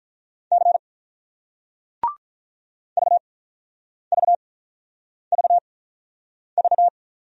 Morse code, heard as a keyed tone, about 47 wpm, spaced as for 9 wpm, 700 Hz. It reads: C VVVV